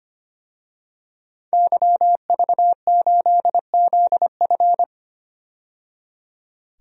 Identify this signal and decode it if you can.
Morse code, heard as a keyed tone, about 25 wpm, 700 Hz. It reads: YV8ZF
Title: DXCC-100 Call Signs - 5 Characters - 25wpm